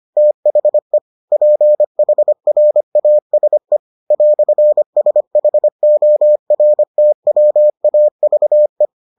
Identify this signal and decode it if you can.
Morse code, heard as a keyed tone, about 25 wpm, 600 Hz. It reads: THE PHRASE "SHORTWAVE